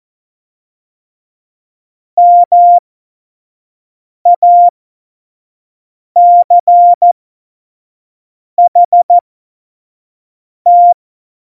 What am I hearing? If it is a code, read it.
Morse code, heard as a keyed tone, about 14 wpm, spaced as for 5 wpm, 700 Hz. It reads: MACHT
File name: Morsekurs / Witze / Witz 26. September 17, Tempo 25